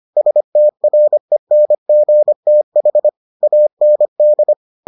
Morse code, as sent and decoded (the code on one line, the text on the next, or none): ... - .-. -. --. - .... / .- -. -..
STRENGTH AND